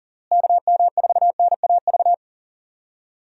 Morse code, heard as a keyed tone, about 40 wpm, 700 Hz. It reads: KM4NAV
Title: USA Call Signs - 6 Characters - Repeated 3x - 40wpm